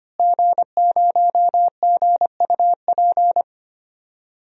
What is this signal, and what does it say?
Morse code, heard as a keyed tone, about 25 wpm, 700 Hz. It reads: G0GUP
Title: DXCC-100 Call Signs - 5 Characters - Repeated 3x - 25wpm